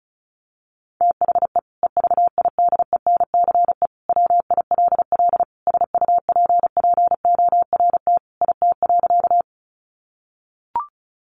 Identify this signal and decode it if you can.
Morse code, heard as a keyed tone, about 34 wpm, 700 Hz. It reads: THE EVIDENCE WILL SUPPORT IT.
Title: Sentences from Top 1300 Words 35wpm